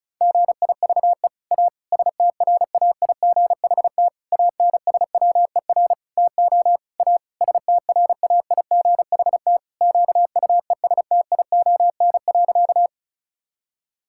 Morse code, heard as a keyed tone, about 36 wpm, 700 Hz. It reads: GIVE A STRAIGHT ANSWER TO A STRAIGHT QUESTION.
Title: Sentences from Top 1000 Words - Review Part 6 - 35wpm